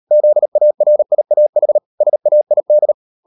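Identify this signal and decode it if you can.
Morse code, heard as a keyed tone, about 38 wpm, 600 Hz. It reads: ZARIAH SAID
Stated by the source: Morse code practice at 38 WPM, sending Ken Follett's novel Never in English, from CW med Gnister.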